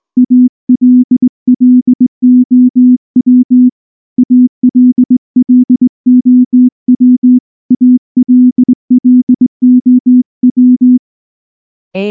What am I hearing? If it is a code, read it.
Morse code, heard as a keyed tone, about 19 wpm, 250 Hz. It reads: ALLOW ALLOW ALLOW
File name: morse / morse-training-20WPM-6